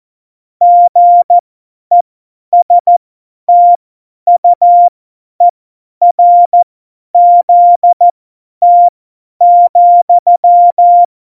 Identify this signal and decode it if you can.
Morse code, heard as a keyed tone, about 14 wpm, spaced as for 10 wpm, 700 Hz. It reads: GESTUERZT,
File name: Morsekurs / Witze / Witz 17. Oktober 17, Tempo 50